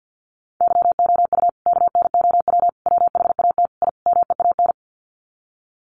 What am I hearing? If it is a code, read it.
Morse code, heard as a keyed tone, about 50 wpm, 700 Hz. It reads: YOU KNOW WHAT I MEAN
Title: Sets of 5 Words 50wpm